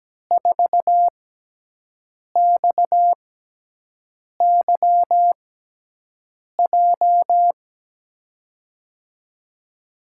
Morse code, heard as a keyed tone, about 17 wpm, spaced as for 6 wpm, 700 Hz. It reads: 4XYJ
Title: USA Call Signs - 6x Character Spacing - 17wpm